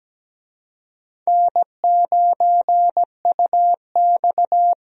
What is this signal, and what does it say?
Morse code, heard as a keyed tone, about 17 wpm, 700 Hz. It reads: N9UX
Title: USA Call Signs - 4 Characters - Repeated 3x - 17wpm